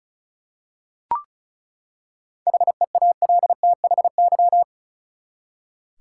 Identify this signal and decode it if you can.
Morse code, heard as a keyed tone, about 34 wpm, 700 Hz. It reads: HEALTHY
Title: Seven Letter Words - 35wpm